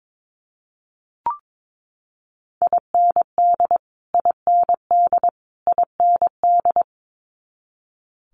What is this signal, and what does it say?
Morse code, heard as a keyed tone, about 22 wpm, 700 Hz. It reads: IND IND IND
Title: Multi-letter Phonemes Repeated 3x 22wpm